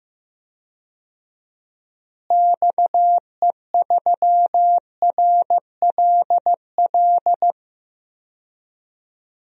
Morse code, heard as a keyed tone, about 15 wpm, 700 Hz. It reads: XE3RLL
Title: DXCC-100 Call Signs - 6 Characters - Repeated 3x - 15wpm